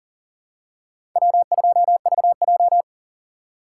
Morse code, heard as a keyed tone, about 40 wpm, 700 Hz. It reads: W2VJ
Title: USA Call Signs - 4 Characters - 2x Character Spacing - 40wpm